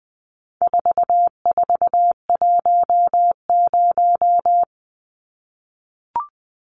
Morse code, heard as a keyed tone, about 20 wpm, 700 Hz. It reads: 4410
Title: Numbers - 4-Digit 20wpm